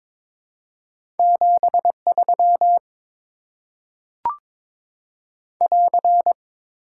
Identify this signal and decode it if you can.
Morse code, heard as a keyed tone, about 21 wpm, 700 Hz. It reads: 73 <AR>
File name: Top 90 QSO Elements - 6x Character Spacing - 22wpm